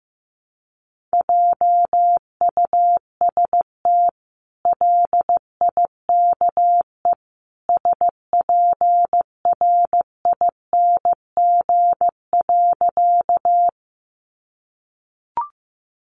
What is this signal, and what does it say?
Morse code, heard as a keyed tone, about 15 wpm, 700 Hz. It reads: JUST LIKE SPRING.
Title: Sentences from Top 1200 Words 15wpm